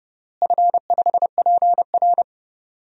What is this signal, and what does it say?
Morse code, heard as a keyed tone, about 30 wpm, 700 Hz. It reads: F5PR
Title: DXCC-100 Call Signs - 4 Characters - Repeated 3x - 30wpm